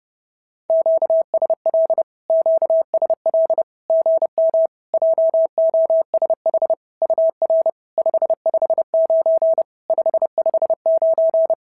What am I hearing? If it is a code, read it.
Morse code, heard as a keyed tone, about 30 wpm, 650 Hz. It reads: QSL QSL GM JOSH UR 559 559
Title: SOTA QSO Practice 30wpm